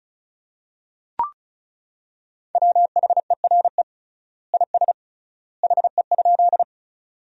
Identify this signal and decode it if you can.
Morse code, heard as a keyed tone, about 36 wpm, 700 Hz. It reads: WHERE IS HE?
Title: 3x Word-Spacing - Sentences from Top 200 Words - 35wpm